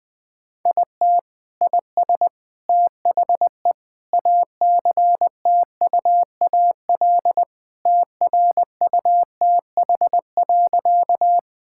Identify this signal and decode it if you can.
Morse code, heard as a keyed tone, about 21 wpm, 700 Hz. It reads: IT IS THE ACTUAL TRUTH.
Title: Sentences from Top 1000 Words - Review Part 5 - 20wpm